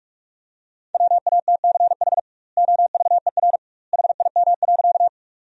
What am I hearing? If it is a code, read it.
Morse code, heard as a keyed tone, about 44 wpm, 700 Hz. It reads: WATCH OVER HIM.
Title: 2x Word-Spacing - Sentences from Top 300 Words - 45wpm